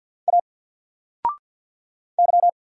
Morse code, heard as a keyed tone, about 50 wpm, 700 Hz. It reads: A Y
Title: Single Letter-Number 50wpm